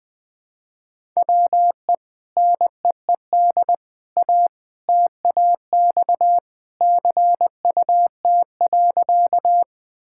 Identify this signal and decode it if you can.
Morse code, heard as a keyed tone, about 21 wpm, 700 Hz. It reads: WE NEED A TAX CUT.
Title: Sentences from Top 1000 Words - Review Part 5 - 20wpm